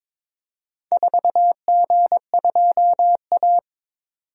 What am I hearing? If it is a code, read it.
Morse code, heard as a keyed tone, about 22 wpm, 700 Hz. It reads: 4G2A
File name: practice / 22wpm / Four Letter-Number Combinations - 22wpm